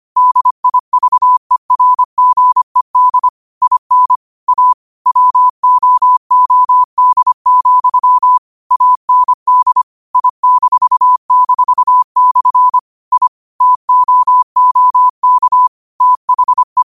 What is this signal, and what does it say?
Morse code, heard as a keyed tone, about 25 wpm, 1000 Hz. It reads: DIVERGED IN A WOOD, AND I--/ I TOOK THE
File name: qotdfiles / 25wpm